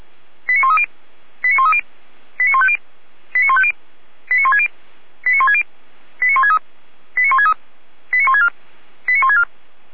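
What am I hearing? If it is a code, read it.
Five-tone selective call (ZVEI), heard as quick runs of five tones, tones of about 70 ms. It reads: 89130 89130 89150 89150 89160 89160 89163 89163 89164 89164
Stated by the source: ZVEI five-tone call received by SDR.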